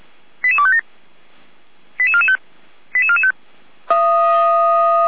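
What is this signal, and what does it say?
Five-tone selective call (ZVEI), heard as quick runs of five tones, tones of about 70 ms. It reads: 88257 88445 88445
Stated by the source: ZVEI five-tone selective call received on an SDR.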